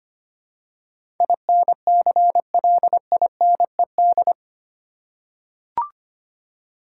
Morse code, heard as a keyed tone, about 25 wpm, 700 Hz. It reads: INCLINED
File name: 8 Letter Words - First 3 Letters Given - 25wpm